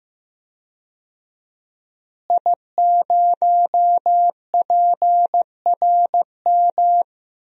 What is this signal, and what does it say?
Morse code, heard as a keyed tone, about 15 wpm, 700 Hz. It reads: I0PRM
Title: DXCC-100 Call Signs - 5 Characters - Repeated 3x - 15wpm